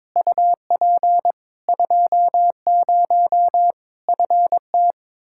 Morse code, heard as a keyed tone, about 22 wpm, 700 Hz. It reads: UP 20 FT
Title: K1USN SST QSO Practice 22wpm